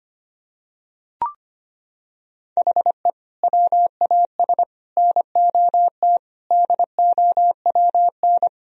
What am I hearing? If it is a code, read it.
Morse code, heard as a keyed tone, about 25 wpm, 700 Hz. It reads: HE WAS NOT DOWN
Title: Sentences from Top 1000 Words - Review Part 1 - 25wpm